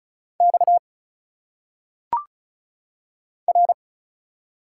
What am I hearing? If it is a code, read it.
Morse code, heard as a keyed tone, about 34 wpm, 700 Hz. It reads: X R
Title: Single Letters - Phonetic Pronunciation 35wpm